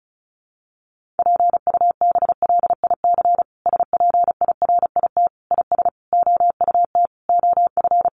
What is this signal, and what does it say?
Morse code, heard as a keyed tone, about 34 wpm, 700 Hz. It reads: PUBLIC SPIRIT IS OUT OF
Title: Sentences from Top 1400 Words 35wpm